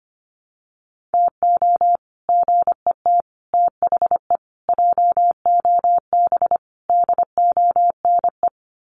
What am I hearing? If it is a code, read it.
Morse code, heard as a keyed tone, about 25 wpm, 700 Hz. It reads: TO GET THE JOB DONE
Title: Sets of 5 Words Encore2 25wpm